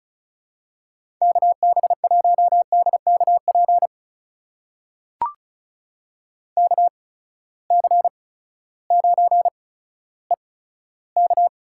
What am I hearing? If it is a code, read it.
Morse code, heard as a keyed tone, about 36 wpm, spaced as for 16 wpm, 700 Hz. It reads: KB1DKP KC9EK
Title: USA Call Signs - 6 Characters - 8x Character Spacing - 35wpm